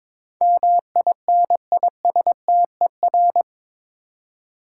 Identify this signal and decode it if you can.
Morse code, heard as a keyed tone, about 22 wpm, 700 Hz. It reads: MINISTER
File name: practice / 22wpm / Eight Letter Words - 22wpm